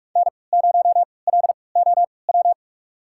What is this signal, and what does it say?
Morse code, heard as a keyed tone, about 45 wpm, spaced as for 25 wpm, 700 Hz. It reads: N0LOW
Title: USA Call Signs - 5 Characters - 3x Character Spacing - 45wpm